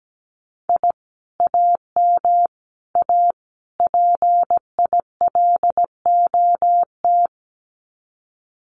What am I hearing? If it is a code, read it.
Morse code, heard as a keyed tone, about 17 wpm, 700 Hz. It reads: I AM A PILOT
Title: Common QSO Phrases Encore 17wpm